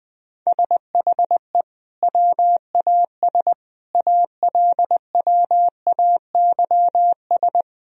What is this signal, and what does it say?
Morse code, heard as a keyed tone, about 20 wpm, 700 Hz. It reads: SHE WAS ALWAYS
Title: Sentences from Top 1000 Words - Review Part 3 - 20wpm